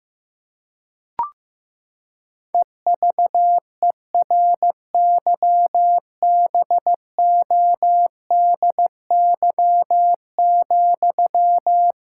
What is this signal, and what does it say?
Morse code, heard as a keyed tone, about 15 wpm, 700 Hz. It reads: EVERYBODY,
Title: Sentences from Top 1600 Words 15wpm